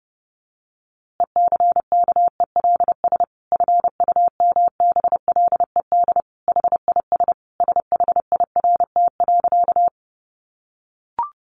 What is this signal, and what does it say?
Morse code, heard as a keyed tone, about 30 wpm, 700 Hz. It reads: ECKELS FUMBLED HIS SHIRT.
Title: A Sound Of Thunder 30wpm